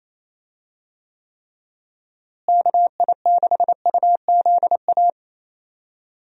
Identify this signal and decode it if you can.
Morse code, heard as a keyed tone, about 28 wpm, 700 Hz. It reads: KI6UZA
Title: USA Call Signs - 6 Characters - Repeated 2x - 28wpm